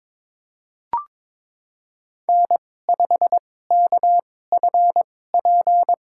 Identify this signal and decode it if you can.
Morse code, heard as a keyed tone, about 22 wpm, spaced as for 16 wpm, 700 Hz. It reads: N5KFP
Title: USA Call Signs - 5 Characters - 2x Character Spacing - 22wpm